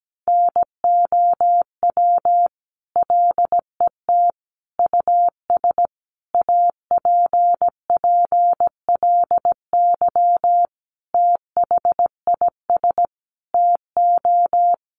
Morse code, heard as a keyed tone, about 17 wpm, 700 Hz. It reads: NOW LET US APPLY THIS TO
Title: Sentences from Top 1200 Words 17wpm